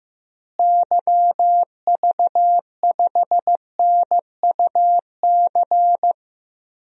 Morse code, heard as a keyed tone, about 15 wpm, 700 Hz. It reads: YV5NUC
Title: DXCC-100 Call Signs - 6 Characters - Repeated 2x - 15wpm